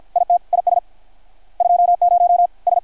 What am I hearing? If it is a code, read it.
Morse code, heard as a keyed tone, about 53 wpm, 660 Hz. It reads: IT IS 20 S
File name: archived / CRSnet